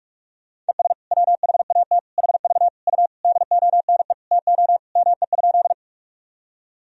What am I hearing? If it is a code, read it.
Morse code, heard as a keyed tone, about 46 wpm, 690 Hz. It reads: ES WHAT HV U DONE TO ME?
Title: Sentences from Top 500 Words and Abbreviations 45wpm